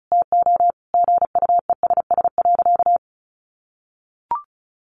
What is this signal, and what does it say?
Morse code, heard as a keyed tone, about 34 wpm, 700 Hz. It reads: TO GUESS.
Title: Sentences from Top 1500 Words 35wpm